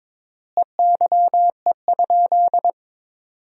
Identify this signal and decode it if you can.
Morse code, heard as a keyed tone, about 22 wpm, 700 Hz. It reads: EYE?
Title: 3x Word-Spacing - Sentences from Top 500 Words - 22wpm